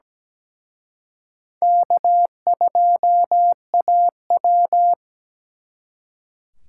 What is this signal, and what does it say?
Morse code, heard as a keyed tone, about 20 wpm, 700 Hz. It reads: K2AW
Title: Speed Racing - USA Call Signs 17wpm